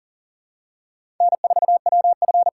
Morse code, heard as a keyed tone, about 40 wpm, 700 Hz. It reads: N4WF
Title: USA Callsign DE Callsign - 4 Characters - 40wpm